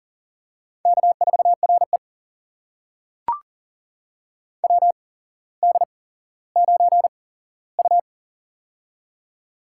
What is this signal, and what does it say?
Morse code, heard as a keyed tone, about 40 wpm, spaced as for 17 wpm, 700 Hz. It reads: K4RE WD9U
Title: USA Call Signs - 4 Characters - 8x Character Spacing - 40wpm